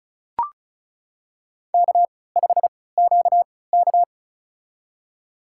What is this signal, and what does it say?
Morse code, heard as a keyed tone, about 34 wpm, spaced as for 20 wpm, 700 Hz. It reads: K5QK
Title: USA Call Signs - 4 Characters - 3x Character Spacing - 35wpm